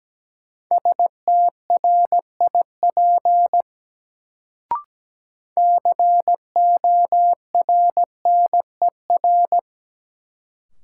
Morse code, heard as a keyed tone, about 17 wpm, 700 Hz. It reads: STRIP CORNER
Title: Sentences from Top 1600 Words - New Word Drill - 17wpm